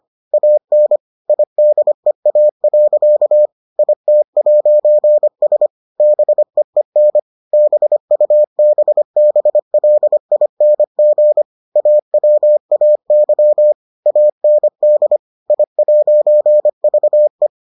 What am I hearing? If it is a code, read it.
Morse code, heard as a keyed tone, about 25 wpm, 600 Hz. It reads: AN IDEA. IT'S BEEN BUBBLING AWAY AND I'VE